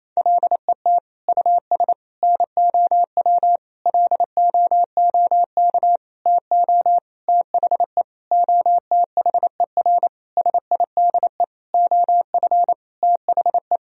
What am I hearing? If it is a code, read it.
Morse code, heard as a keyed tone, about 28 wpm, 700 Hz. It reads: LET US NOW LOOK TO THE OTHER SIDE OF THE
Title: Sentences from Top 1000 Words - Review Part 1 - 28wpm